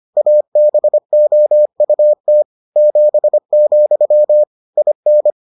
Morse code, heard as a keyed tone, about 25 wpm, 600 Hz. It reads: ABOUT 7, IN